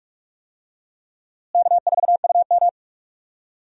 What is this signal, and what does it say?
Morse code, heard as a keyed tone, about 46 wpm, 700 Hz. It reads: K4UM
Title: USA Call Signs - 4 Characters - Repeated 3x - 45wpm